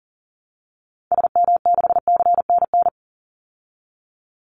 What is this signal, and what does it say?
Morse code, heard as a keyed tone, about 40 wpm, 700 Hz. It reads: SM6CNN
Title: Call Signs Repeated 2x - 40wpm